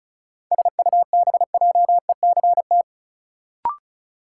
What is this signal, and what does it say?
Morse code, heard as a keyed tone, about 34 wpm, 700 Hz. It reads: SUBJECT
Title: Sentences from Top 800 Words - New Word Drill - 35wpm